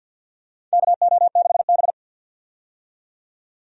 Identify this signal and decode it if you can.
Morse code, heard as a keyed tone, about 50 wpm, 700 Hz. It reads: KO6B
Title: USA Call Signs - 4 Characters - 50wpm